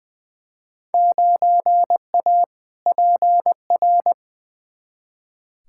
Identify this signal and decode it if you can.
Morse code, heard as a keyed tone, about 21 wpm, 700 Hz. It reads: 9A PR
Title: Field Day Exchanges Encore - 20wpm